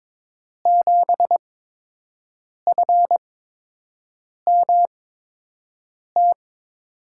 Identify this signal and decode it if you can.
Morse code, heard as a keyed tone, about 22 wpm, spaced as for 6 wpm, 700 Hz. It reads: 7FMT
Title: USA Call Signs - 8x Character Spacing - 22wpm